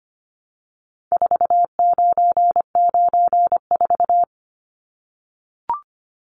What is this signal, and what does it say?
Morse code, heard as a keyed tone, about 25 wpm, 700 Hz. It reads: 4994
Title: Numbers - 4-Digit 25wpm